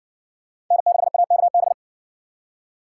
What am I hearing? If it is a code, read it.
Morse code, heard as a keyed tone, about 60 wpm, 700 Hz. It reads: N6AKB